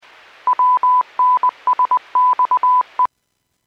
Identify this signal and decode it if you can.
Morse code, heard as a keyed tone, about 20 wpm, 1000 Hz. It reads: WNSXE
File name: Businesses / 03138 00-54 Businesses 451.25 WPS